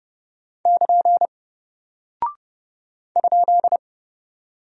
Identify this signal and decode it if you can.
Morse code, heard as a keyed tone, about 28 wpm, 700 Hz. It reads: <KN> ?
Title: Single Character (Letter:Number:Punctuation:Prosign) Alt 30wpm